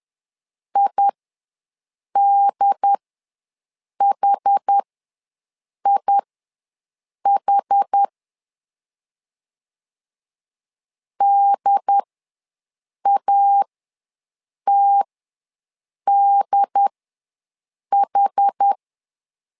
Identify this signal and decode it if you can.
Morse code, heard as a keyed tone, about 10 wpm, spaced as for 6 wpm, 800 Hz. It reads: IDHIH DATDH
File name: LEZIONI / 2 UHDN / 2 UHD ws 10wpm 2058